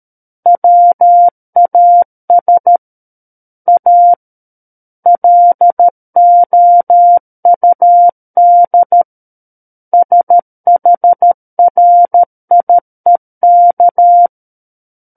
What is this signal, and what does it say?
Morse code, heard as a keyed tone, about 13 wpm, 700 Hz. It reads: WAS A LOUD SHRIEK
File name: Book One / 04-Chapter 4 - 13 WPM